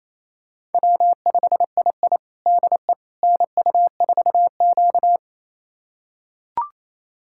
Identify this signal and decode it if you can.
Morse code, heard as a keyed tone, about 27 wpm, 690 Hz. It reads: W5II DE NU4Q
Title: USA Callsign DE Callsign - 4 Characters - 28wpm